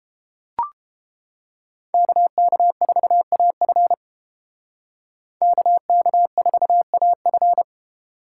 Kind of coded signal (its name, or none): Morse code